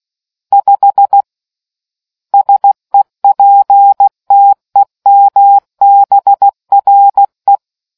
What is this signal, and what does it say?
Morse code, heard as a keyed tone, about 16 wpm, 800 Hz. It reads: 5 SEPTEMBRE